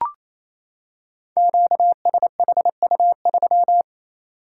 Morse code, heard as a keyed tone, about 27 wpm, 700 Hz. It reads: QSHU3
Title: Code Groups - 5 Letter:Numbers - 28wpm